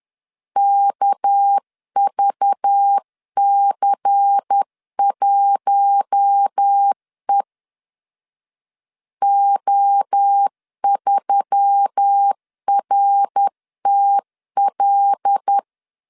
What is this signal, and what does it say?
Morse code, heard as a keyed tone, about 10 wpm, 800 Hz. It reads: KVC1E O3RTL
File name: CW / DET / 1 6 1500C S C 10wpm 4252